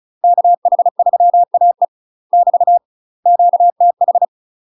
Morse code, heard as a keyed tone, about 34 wpm, 700 Hz. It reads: KH3AE <BT> QTH
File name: UK MASTER QSOs TRAINING/35WPM